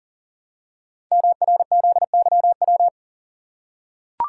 Morse code, heard as a keyed tone, about 41 wpm, 700 Hz. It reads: MRZYW
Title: Code Groups - 5 Letters - 40wpm